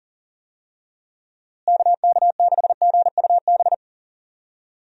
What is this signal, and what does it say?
Morse code, heard as a keyed tone, about 40 wpm, 700 Hz. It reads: KK6GUB
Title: USA Call Signs - 6 Characters - Repeated 3x - 40wpm